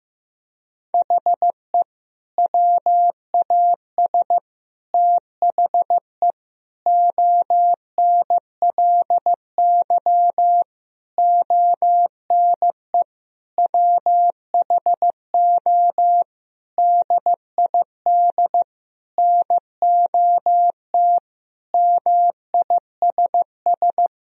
Morse code, heard as a keyed tone, about 15 wpm, 700 Hz. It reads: HE WAS THE ONLY ONE WHO DID NOT MISS
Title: Sentences from Top 1000 Words - Review Part 3 - 15wpm